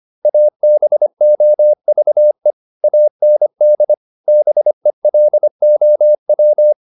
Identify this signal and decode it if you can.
Morse code, heard as a keyed tone, about 25 wpm, 600 Hz. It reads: ABOVE AND BELOW
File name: PR403 25